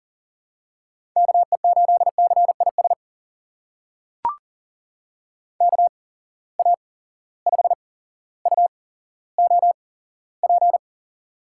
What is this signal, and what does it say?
Morse code, heard as a keyed tone, about 38 wpm, spaced as for 17 wpm, 700 Hz. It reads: KE8CIS KA5UOP
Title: USA Call Signs - 6 Characters - 8x Character Spacing - 40wpm